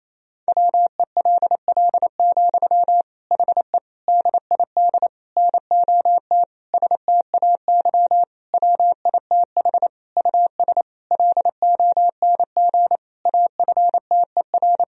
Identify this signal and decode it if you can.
Morse code, heard as a keyed tone, about 28 wpm, 700 Hz. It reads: WELL, HE DID NOT STAY WITH US LONG AFTER
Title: Sentences from Top 1000 Words - Review Part 2 - 28wpm